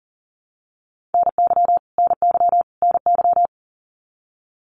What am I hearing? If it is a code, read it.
Morse code, heard as a keyed tone, about 40 wpm, 700 Hz. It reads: NY NY NY
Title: US States Repeated 3x 40wpm